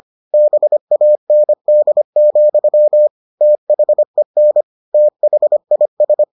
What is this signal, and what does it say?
Morse code, heard as a keyed tone, about 25 wpm, 600 Hz. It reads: BAND, THEN THIS